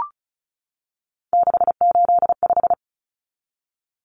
Morse code, heard as a keyed tone, about 34 wpm, 700 Hz. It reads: 685
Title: Numbers - 3-Digit 35wpm